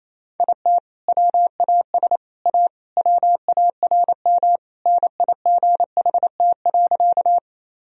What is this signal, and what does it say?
Morse code, heard as a keyed tone, about 28 wpm, 700 Hz. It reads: IT WAS A WARM NIGHT.
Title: Sentences from Top 1000 Words - Review Part 2 - 28wpm